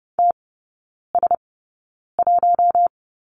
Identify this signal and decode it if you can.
Morse code, heard as a keyed tone, about 30 wpm, spaced as for 9 wpm, 700 Hz. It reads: TS1
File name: course / 30:10wpm / 087 Lesson 15e - Number 5 - Three Characters 30wpm